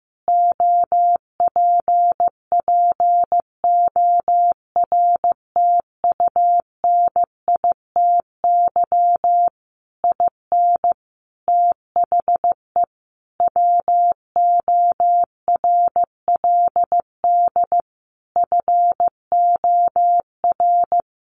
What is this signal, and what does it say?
Morse code, heard as a keyed tone, about 15 wpm, 700 Hz. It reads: OPPORTUNITY IN THE WORLD FOR
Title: Sentences from Top 1500 Words 15wpm